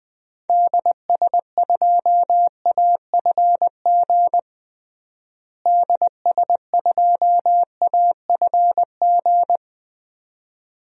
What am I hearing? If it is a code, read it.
Morse code, heard as a keyed tone, about 20 wpm, 700 Hz. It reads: DS2AFG DS2AFG
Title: DXCC-100 Call Signs - 6 Characters - Repeated 2x - 20wpm